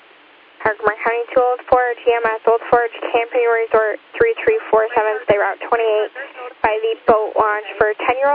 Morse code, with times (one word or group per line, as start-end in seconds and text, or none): none